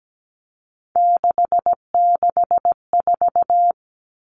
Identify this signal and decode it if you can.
Morse code, heard as a keyed tone, about 17 wpm, 700 Hz. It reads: 664
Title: Common Cut Numbers - 3-Digit 17wpm